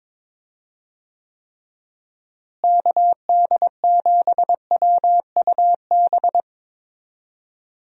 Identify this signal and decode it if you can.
Morse code, heard as a keyed tone, about 22 wpm, 700 Hz. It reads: KD7WUB
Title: USA Call Signs - 6 Characters - Repeated 3x - 22wpm